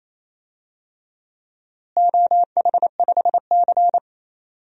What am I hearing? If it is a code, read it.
Morse code, heard as a keyed tone, about 28 wpm, 700 Hz. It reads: OH5C
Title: DXCC-100 Call Signs - 4 Characters - Repeated 3x - 28wpm